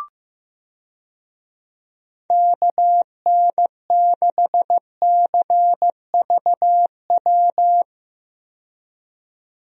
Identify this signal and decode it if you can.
Morse code, heard as a keyed tone, about 15 wpm, 700 Hz. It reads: KN6CVW
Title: USA Call Signs - 6 Characters - 15wpm